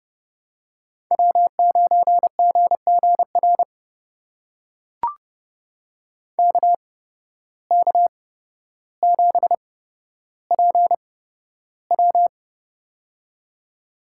Morse code, heard as a keyed tone, about 30 wpm, spaced as for 13 wpm, 700 Hz. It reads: W9GGR KK7PW
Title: USA Call Signs - 5 Characters - 8x Character Spacing - 30wpm